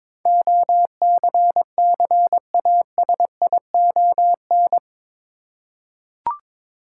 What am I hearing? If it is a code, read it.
Morse code, heard as a keyed tone, about 22 wpm, 700 Hz. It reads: OCCASION
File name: practice / 22wpm / Eight Letter Words - 22wpm